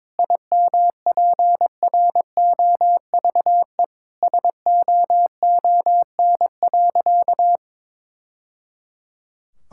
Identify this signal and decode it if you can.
Morse code, heard as a keyed tone, about 22 wpm, 700 Hz. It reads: IMPROVE SOON.
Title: Sentences from Top 1000 Words - Review Part 5 - 22wpm